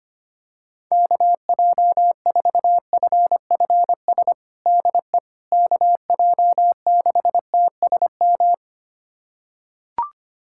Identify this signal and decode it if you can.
Morse code, heard as a keyed tone, about 25 wpm, 700 Hz. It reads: KJ4FFS DE KJ6TSM
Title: USA Callsign DE Callsign - 6 Characters - 25wpm